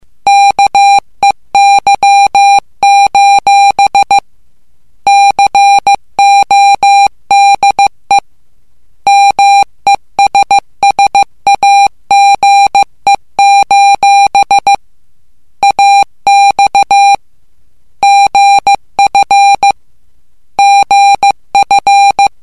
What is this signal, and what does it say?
Morse code, heard as a keyed tone, about 15 wpm, 800 Hz. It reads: KEY: CODE MESSAGE: AX GF GF
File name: morse / english